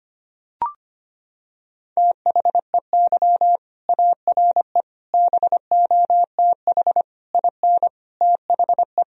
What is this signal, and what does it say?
Morse code, heard as a keyed tone, about 25 wpm, 700 Hz. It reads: THEY ARE BOTH IN THE
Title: Sentences from Top 1000 Words - Review Part 1 - 25wpm